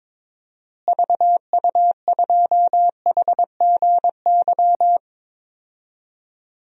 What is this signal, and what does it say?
Morse code, heard as a keyed tone, about 22 wpm, 700 Hz. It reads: VU2HGY
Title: DXCC-100 Call Signs - 6 Characters - Repeated 3x - 22wpm